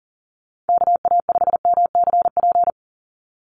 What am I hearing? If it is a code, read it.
Morse code, heard as a keyed tone, about 40 wpm, 700 Hz. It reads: KA5MCP
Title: USA Call Signs 40wpm